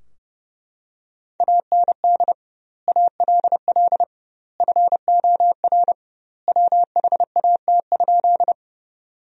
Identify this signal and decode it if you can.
Morse code, heard as a keyed tone, about 29 wpm, 700 Hz. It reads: AND ALL FOR WHAT?
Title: 2x Word-Spacing - Sentences from Top 100 Words - 30wpm